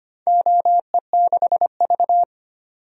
Morse code, heard as a keyed tone, about 25 wpm, 700 Hz. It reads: OE6V
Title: DXCC-100 Call Signs - 4 Characters - 25wpm